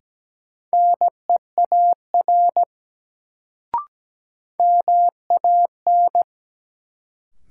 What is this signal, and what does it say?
Morse code, heard as a keyed tone, about 17 wpm, 700 Hz. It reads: NEAR MAN
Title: Sentences from Top 200 Words - New Word Drill - 17wpm